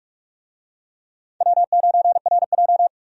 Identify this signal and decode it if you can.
Morse code, heard as a keyed tone, about 46 wpm, 700 Hz. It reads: W9RJ
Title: USA Call Signs - 4 Characters - 45wpm